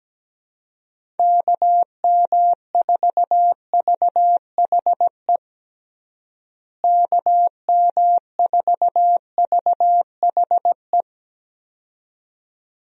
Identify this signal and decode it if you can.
Morse code, heard as a keyed tone, about 17 wpm, 700 Hz. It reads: KM4VHE KM4VHE
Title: USA Call Signs - 6 Characters - Repeated 2x - 17wpm